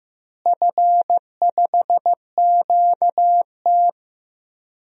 Morse code, heard as a keyed tone, about 15 wpm, 700 Hz. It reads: F5QT
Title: DXCC-100 Call Signs - 4 Characters - Repeated 3x - 15wpm